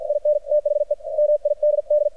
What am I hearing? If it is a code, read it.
Morse code, heard as a keyed tone, about 53 wpm, 600 Hz, with radio noise. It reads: IN THE MIDD
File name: archived / CRSnet